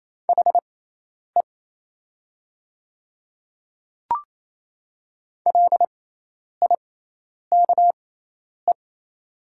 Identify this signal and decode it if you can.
Morse code, heard as a keyed tone, about 27 wpm, spaced as for 10 wpm, 700 Hz. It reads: HE LIKE
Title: 6x Character Spacing - Top 100 Words - 28wpm